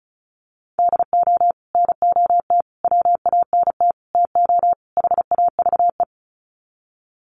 Morse code, heard as a keyed tone, about 34 wpm, 700 Hz. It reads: DO NOT WANT TO HAVE
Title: Sets of 5 Words 35wpm